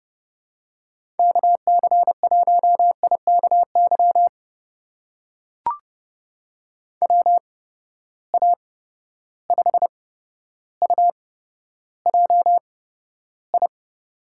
Morse code, heard as a keyed tone, about 32 wpm, spaced as for 13 wpm, 700 Hz. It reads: KC1IKY WA5UJI
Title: USA Call Signs - 6 Characters - 8x Character Spacing - 30wpm